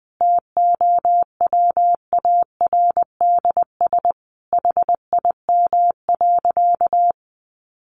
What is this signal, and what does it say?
Morse code, heard as a keyed tone, about 20 wpm, 700 Hz. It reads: TOWARDS HIM.
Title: Sentences from Top 1100 Words 20wpm